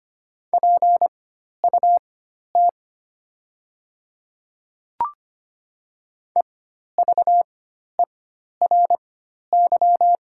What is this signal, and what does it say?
Morse code, heard as a keyed tone, about 25 wpm, spaced as for 12 wpm, 700 Hz. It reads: PUT EVERY
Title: 4x Character Spacing - Top 200 Words - 25wpm